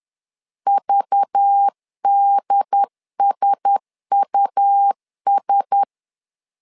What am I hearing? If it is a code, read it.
Morse code, heard as a keyed tone, about 10 wpm, 800 Hz. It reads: VDSUS
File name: LEZIONI / 5 LBVC / 5 LBV 10wpm 2248